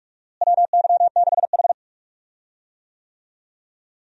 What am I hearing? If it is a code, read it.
Morse code, heard as a keyed tone, about 43 wpm, 700 Hz. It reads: WY6H
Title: USA Call Signs - 4 Characters - Repeated 3x - 45wpm